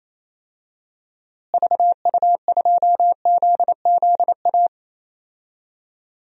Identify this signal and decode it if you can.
Morse code, heard as a keyed tone, about 28 wpm, 700 Hz. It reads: VU2ZZA
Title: DXCC-100 Call Signs - 6 Characters - Repeated 3x - 28wpm